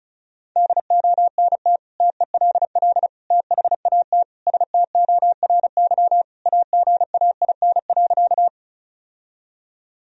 Morse code, heard as a keyed tone, about 36 wpm, 700 Hz. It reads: DONT TELL THAT STORY AGAIN.
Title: Sentences from Top 1000 Words - Review Part 1 - 35wpm